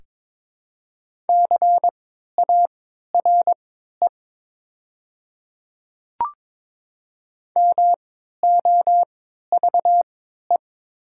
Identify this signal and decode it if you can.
Morse code, heard as a keyed tone, about 22 wpm, spaced as for 13 wpm, 700 Hz. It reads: CARE MOVE
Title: 3x Character Spacing - Top 300 Words - 22wpm